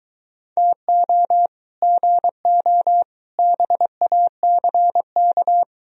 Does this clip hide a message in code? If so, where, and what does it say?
Yes: Morse code from 0.6 to 5.6 s, heard as a keyed tone, about 23 wpm, 700 Hz. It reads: TO GO BACK